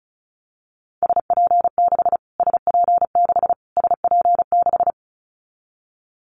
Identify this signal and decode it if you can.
Morse code, heard as a keyed tone, about 34 wpm, 700 Hz. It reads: SP6 SP6 SP6
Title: Callsign Prefixes for Top 25 DXCC - Repeated 3x - 35wpm